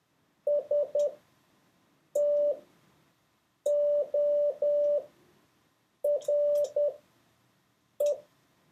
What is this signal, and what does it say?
Morse code, heard as a keyed tone, about 10 wpm, spaced as for 6 wpm, 580 Hz. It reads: STORE
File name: 4of6-words-1